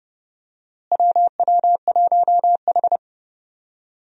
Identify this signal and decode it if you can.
Morse code, heard as a keyed tone, about 30 wpm, 700 Hz. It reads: WW1H
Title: USA Call Signs - 4 Characters - Repeated 3x - 30wpm